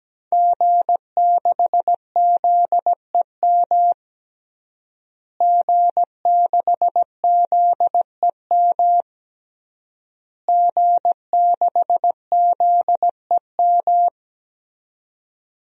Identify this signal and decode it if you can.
Morse code, heard as a keyed tone, about 17 wpm, 700 Hz. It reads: G6ZEM G6ZEM G6ZEM
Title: DXCC-100 Call Signs - 5 Characters - Repeated 3x - 17wpm